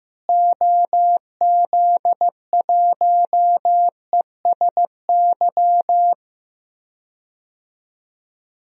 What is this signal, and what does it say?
Morse code, heard as a keyed tone, about 15 wpm, 700 Hz. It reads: OZ1ESY
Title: DXCC-100 Call Signs - 6 Characters - Repeated 2x - 15wpm